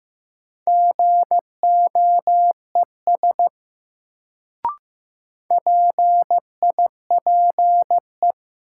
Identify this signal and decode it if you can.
Morse code, heard as a keyed tone, about 16 wpm, 700 Hz. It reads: GOES PIPE
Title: Sentences from Top 1500 Words - New Word Drill Alt - 15wpm